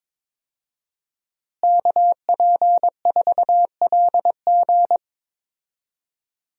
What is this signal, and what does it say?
Morse code, heard as a keyed tone, about 22 wpm, 700 Hz. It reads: KP4LG
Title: DXCC-100 Call Signs - 5 Characters - Repeated 2x - 22wpm